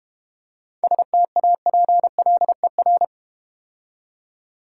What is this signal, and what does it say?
Morse code, heard as a keyed tone, about 32 wpm, 690 Hz. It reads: STAPLER